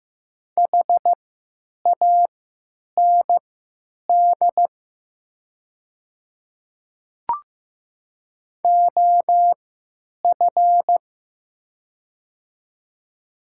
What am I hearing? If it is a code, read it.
Morse code, heard as a keyed tone, about 15 wpm, spaced as for 9 wpm, 700 Hz. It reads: HAND OF
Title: 3x Character Spacing - Top 200 Words - 15wpm